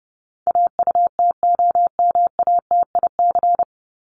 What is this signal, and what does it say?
Morse code, heard as a keyed tone, about 30 wpm, 700 Hz. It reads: AUTOMATIC
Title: Top 1751-2000 Words 30wpm